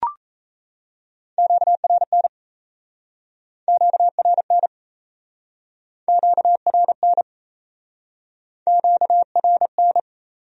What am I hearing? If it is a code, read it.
Morse code, heard as a keyed tone, about 39 wpm, 700 Hz. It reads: QRN QRN QRN QRN